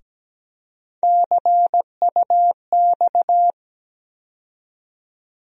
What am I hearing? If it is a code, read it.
Morse code, heard as a keyed tone, about 17 wpm, 700 Hz. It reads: CUX
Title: USA Call Sign Postfixes - 3 Letters 17wpm